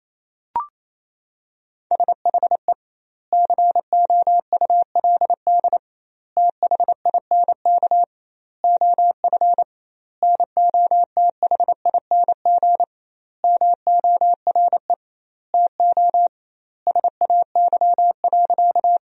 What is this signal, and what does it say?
Morse code, heard as a keyed tone, about 28 wpm, 700 Hz. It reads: SHE COULD THINK OF NOTHING MORE TO SAY.
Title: 2x Word-Spacing - Sentences from Top 500 Words - 28wpm